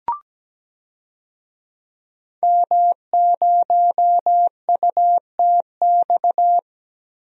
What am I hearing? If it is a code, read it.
Morse code, heard as a keyed tone, about 17 wpm, 700 Hz. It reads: M0UTX
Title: DXCC-100 Call Signs - 5 Characters - Repeated 2x - 17wpm